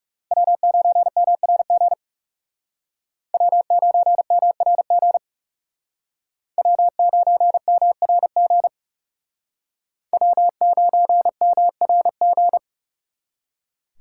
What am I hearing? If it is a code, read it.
Morse code, heard as a keyed tone, about 39 wpm, 700 Hz. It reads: W9MRG W9MRG W9MRG W9MRG